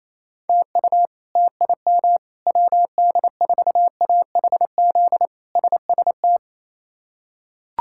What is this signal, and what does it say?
Morse code, heard as a keyed tone, about 28 wpm, 700 Hz. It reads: TU TIM WD4AHZ SST
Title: K1USN SST QSO Practice - 28wpm